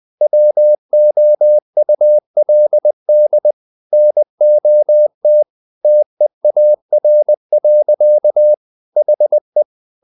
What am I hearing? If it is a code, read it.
Morse code, heard as a keyed tone, about 20 wpm, 600 Hz. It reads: WOULD NOT TEAR. HE
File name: the old man and the sea part 2 20wpm